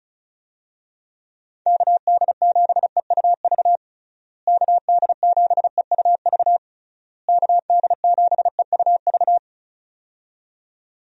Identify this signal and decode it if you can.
Morse code, heard as a keyed tone, about 34 wpm, 700 Hz. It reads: KD7EUV KD7EUV KD7EUV
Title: USA Call Signs - 6 Characters - Repeated 3x - 35wpm